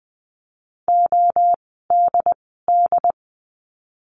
Morse code, heard as a keyed tone, about 20 wpm, 700 Hz. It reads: O D D
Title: 068 Lesson 12f - Letter d - Words 20wpm